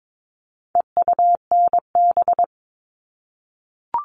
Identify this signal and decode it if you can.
Morse code, heard as a keyed tone, about 22 wpm, 700 Hz. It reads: EUNB
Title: All Cut Numbers - 4-Digit 22wpm